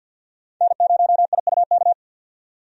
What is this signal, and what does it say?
Morse code, heard as a keyed tone, about 52 wpm, 700 Hz. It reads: N0IUK
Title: USA Call Signs - 5 Characters - 50wpm